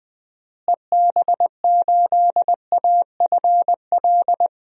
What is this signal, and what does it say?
Morse code, heard as a keyed tone, about 20 wpm, 700 Hz. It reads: EB8AFL
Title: DXCC-100 Call Signs - 6 Characters - Repeated 2x - 20wpm